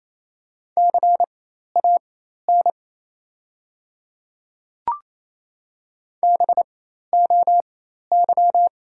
Morse code, heard as a keyed tone, about 28 wpm, spaced as for 13 wpm, 690 Hz. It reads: CAN BOY
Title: 4x Character Spacing - Top 200 Words - 28wpm